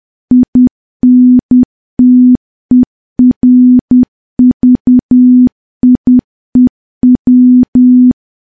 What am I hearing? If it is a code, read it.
Morse code, heard as a keyed tone, about 10 wpm, 250 Hz. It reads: INTERVIEW
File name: morse / morse-training-10WPM